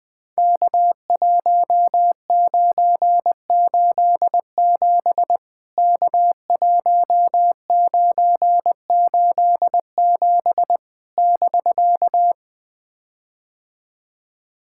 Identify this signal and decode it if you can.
Morse code, heard as a keyed tone, about 21 wpm, 700 Hz. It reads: K1987 K1987 <BK>
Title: POTA QSO Practice - 20wpm